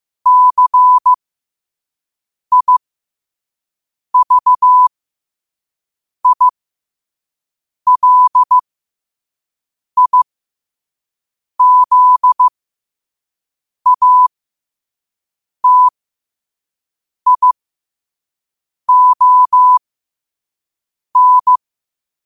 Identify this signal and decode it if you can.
Morse code, heard as a keyed tone, about 15 wpm, spaced as for 5 wpm, 1000 Hz. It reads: CIVILIZATION
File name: qotdfiles / 5wpm